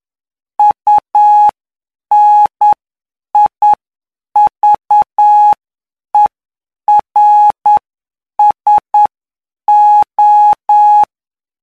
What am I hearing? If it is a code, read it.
Morse code, heard as a keyed tone, about 9 wpm, spaced as for 7 wpm, 840 Hz. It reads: UNIVERSO
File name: cw / py2lcd / 17-PY2LCD-Li��o 26